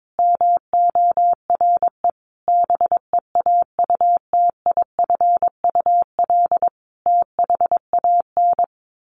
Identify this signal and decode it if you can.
Morse code, heard as a keyed tone, about 22 wpm, 700 Hz. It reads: MORE BEAUTIFUL THAN
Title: Sentences from Top 1200 Words 22wpm